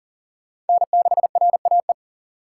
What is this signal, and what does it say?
Morse code, heard as a keyed tone, about 48 wpm, 700 Hz. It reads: N6RAE
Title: Speed Racing - USA 1x3 Call Signs 40wpm